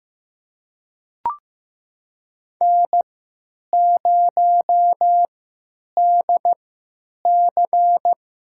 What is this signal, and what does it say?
Morse code, heard as a keyed tone, about 15 wpm, spaced as for 9 wpm, 700 Hz. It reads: N0DC
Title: USA Call Signs - 3x Character Spacing - 15wpm